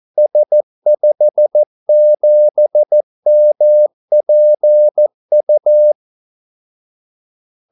Morse code, heard as a keyed tone, about 14 wpm, 600 Hz. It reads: S57MPU
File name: Callsigns-14wpm